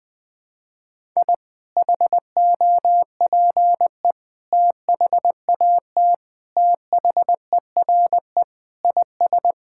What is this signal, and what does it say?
Morse code, heard as a keyed tone, about 20 wpm, 700 Hz. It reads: I HOPE THAT THERE IS
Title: Sentences with 2-Word Phrases - Spoken Beginning - 20wpm